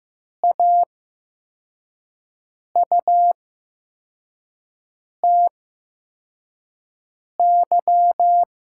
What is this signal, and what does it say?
Morse code, heard as a keyed tone, about 15 wpm, spaced as for 4 wpm, 700 Hz. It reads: AUTY
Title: 8x Character Spacing - Top 500 Words - 15wpm